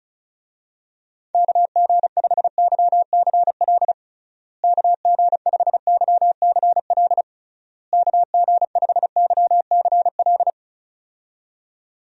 Morse code, heard as a keyed tone, about 34 wpm, 700 Hz. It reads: KG5YCL KG5YCL KG5YCL
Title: USA Call Signs - 6 Characters - Repeated 3x - 35wpm